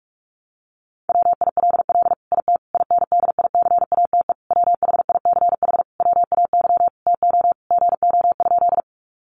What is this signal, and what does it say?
Morse code, heard as a keyed tone, about 44 wpm, 700 Hz. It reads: WILL IT INDICATE WHICH WAY TO GO?
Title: Sentences from Top 1300 Words 45wpm